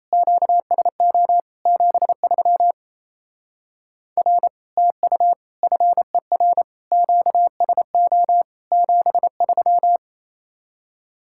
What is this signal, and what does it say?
Morse code, heard as a keyed tone, about 33 wpm, 700 Hz. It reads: QSO 73 R TU FER QSO 73
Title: Speed-Racing Common QSO Phrases - 28wpm